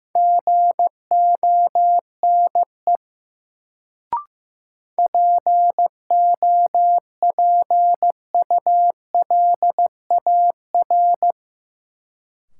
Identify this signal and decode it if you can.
Morse code, heard as a keyed tone, about 15 wpm, 700 Hz. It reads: GONE POPULAR
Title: Sentences from Top 1200 Words - New Word Drill Alt - 15wpm